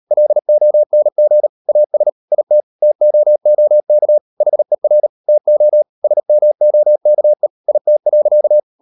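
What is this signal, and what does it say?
Morse code, heard as a keyed tone, about 38 wpm, 600 Hz. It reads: LONG AS IT TOOK HER TO SMOKE IT.
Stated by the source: Morse code practice at 38 WPM, sending Ken Follett's novel Never in English, from CW med Gnister.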